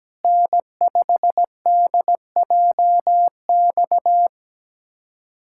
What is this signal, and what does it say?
Morse code, heard as a keyed tone, about 17 wpm, 700 Hz. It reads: N5DJX
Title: USA Callsign DE Callsign 17wpm